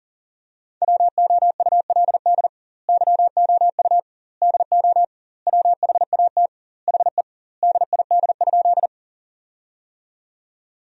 Morse code, heard as a keyed tone, about 40 wpm, 700 Hz. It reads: WOULD YOU DO WHAT HE DID?
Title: 2x Word-Spacing - Sentences from Top 100 Words - 40wpm